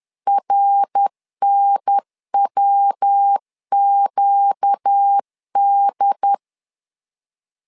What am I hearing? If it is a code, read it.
Morse code, heard as a keyed tone, about 10 wpm, 800 Hz. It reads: RNWQD
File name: LEZIONI / 4 Q1S 10wpm 2133